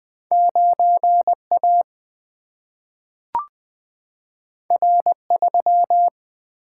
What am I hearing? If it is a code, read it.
Morse code, heard as a keyed tone, about 20 wpm, 700 Hz. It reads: 9 A R 3
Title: Single-Letter Single-Number 20wpm